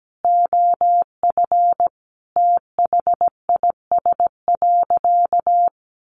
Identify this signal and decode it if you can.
Morse code, heard as a keyed tone, about 17 wpm, 700 Hz. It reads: OF THIS.
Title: Sentences from Top 1100 Words 17wpm